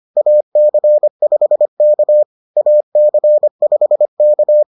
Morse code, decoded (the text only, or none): AC5K AC5K